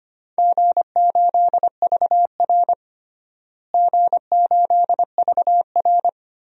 Morse code, heard as a keyed tone, about 25 wpm, 700 Hz. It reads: G8VR G8VR
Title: DXCC-100 Call Signs - 4 Characters - Repeated 2x - 25wpm